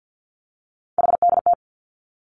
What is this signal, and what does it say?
Morse code, heard as a keyed tone, about 50 wpm, 700 Hz. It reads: HDT
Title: Three Challenging Letter-Number Combinations 50wpm